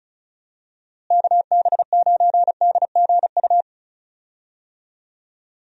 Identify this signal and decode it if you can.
Morse code, heard as a keyed tone, about 34 wpm, 700 Hz. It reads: KB9DGU
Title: USA Call Signs - 6 Characters - Repeated 3x - 35wpm